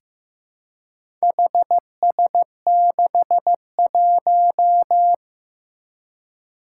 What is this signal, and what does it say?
Morse code, heard as a keyed tone, about 15 wpm, 700 Hz. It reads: HS61
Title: Four Challenging Letter-Number Combinations 15wpm